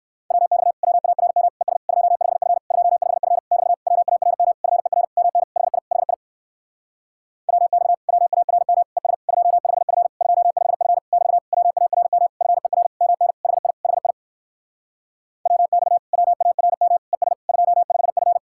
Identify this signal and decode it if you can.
Morse code, heard as a keyed tone, about 59 wpm, 700 Hz. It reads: WX WARM ES 25F 25F <BT> WARM FER MN HI HI WX WARM ES 25F 25F <BT> WARM FER MN HI HI WX WARM ES 25F